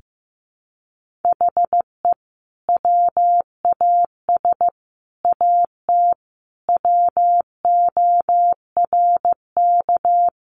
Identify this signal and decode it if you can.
Morse code, heard as a keyed tone, about 15 wpm, 700 Hz. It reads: HE WAS AT WORK
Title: Sentences from Top 200 Words 15wpm